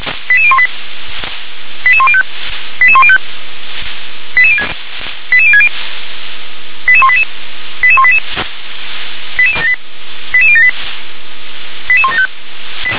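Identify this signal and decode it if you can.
Five-tone selective call (ZVEI), heard as quick runs of five tones, tones of about 70 ms. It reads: 88018 88105 88105 88060 88188 88188 88078 88175
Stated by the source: Five-tone ZVEI selective call 88078, received by SDR.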